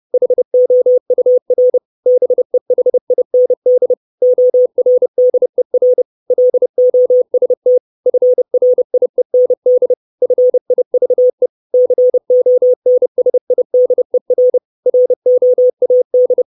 Morse code, heard as a keyed tone, about 30 wpm, 500 Hz. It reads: HOUR BEHIND ORDER LOST FRIEND FIVE CONSIDER ROAD
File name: 632 Words 30WPM